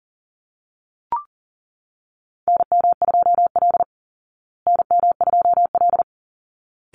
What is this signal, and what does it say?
Morse code, heard as a keyed tone, about 40 wpm, 700 Hz. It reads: NM2L NM2L
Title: Call Signs Repeated 3x - 40wpm